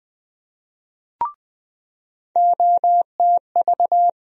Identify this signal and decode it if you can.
Morse code, heard as a keyed tone, about 20 wpm, 700 Hz. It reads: OTV
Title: USA Call Sign Postfixes - 3 Letters 20wpm